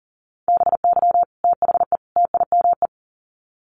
Morse code, heard as a keyed tone, about 40 wpm, 700 Hz. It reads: BY THE TIME
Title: Sets of 3 Words 40wpm